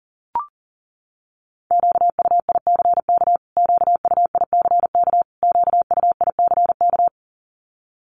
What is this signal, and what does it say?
Morse code, heard as a keyed tone, about 40 wpm, 700 Hz. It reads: QUICK QUICK QUICK
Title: Top 500 Words Repeated 40wpm